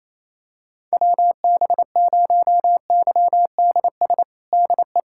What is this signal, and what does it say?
Morse code, heard as a keyed tone, about 28 wpm, 700 Hz. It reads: WB0YDS DE
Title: USA Callsign DE Callsign - 6 Characters - 28wpm